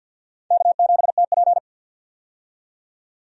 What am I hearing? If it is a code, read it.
Morse code, heard as a keyed tone, about 52 wpm, 700 Hz. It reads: K7TP
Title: USA Call Signs - 4 Characters - 50wpm